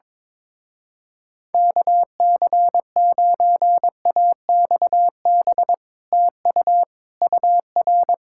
Morse code, heard as a keyed tone, about 22 wpm, 700 Hz. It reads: KC9AXB TU UR
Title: Speed-Racing Common QSO Phrases Encore - 22wpm